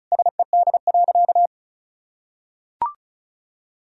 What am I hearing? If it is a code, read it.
Morse code, heard as a keyed tone, about 36 wpm, 700 Hz. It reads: SED.
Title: Sentences from Top 500 Words and Abbreviations 35wpm